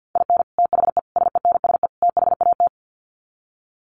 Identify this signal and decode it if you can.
Morse code, heard as a keyed tone, about 50 wpm, 700 Hz. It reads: IN THE SENSE THAT
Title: Sets of 4 Words 50wpm